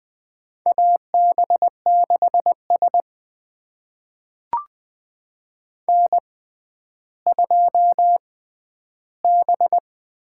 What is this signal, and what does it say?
Morse code, heard as a keyed tone, about 21 wpm, spaced as for 12 wpm, 700 Hz. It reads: AB6S N2B
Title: USA Call Signs - 4 Characters - 6x Character Spacing - 20wpm